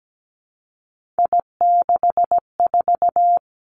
Morse code, heard as a keyed tone, about 17 wpm, 700 Hz. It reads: I64
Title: Three Challenging Letter-Number Combinations 17wpm